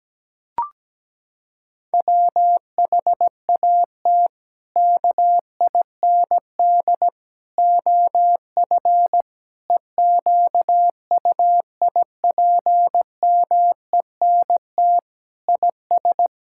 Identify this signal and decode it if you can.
Morse code, heard as a keyed tone, about 17 wpm, 700 Hz. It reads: WHAT KIND OF EQUIPMENT IS
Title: Sentences from Top 1000 Words - Review Part 5 - 17wpm